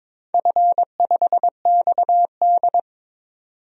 Morse code, heard as a keyed tone, about 22 wpm, 700 Hz. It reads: F5XD
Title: DXCC-100 Call Signs - 4 Characters - 22wpm